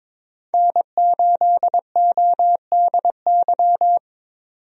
Morse code, heard as a keyed tone, about 21 wpm, 700 Hz. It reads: N8ODY
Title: USA Call Signs - 5 Characters - 22wpm